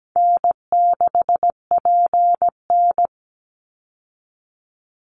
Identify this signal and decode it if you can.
Morse code, heard as a keyed tone, about 17 wpm, 700 Hz. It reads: N6PN
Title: Easy Call Signs Repeated 2x - 17wpm